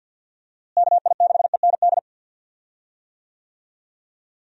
Morse code, heard as a keyed tone, about 51 wpm, 700 Hz. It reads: KI6END
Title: USA Call Signs - 6 Characters - Repeated 3x - 50wpm